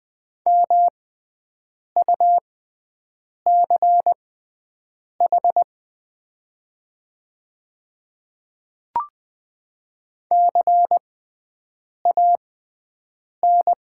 Morse code, heard as a keyed tone, about 20 wpm, spaced as for 7 wpm, 700 Hz. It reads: MUCH CAN
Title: 6x Character Spacing - Top 300 Words - 20wpm